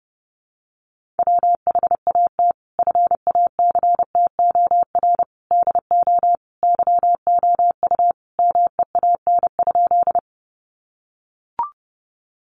Morse code, heard as a keyed tone, about 30 wpm, 700 Hz. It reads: WHAT FACTOR DO YOU MEAN?
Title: Sentences from Top 1300 Words 30wpm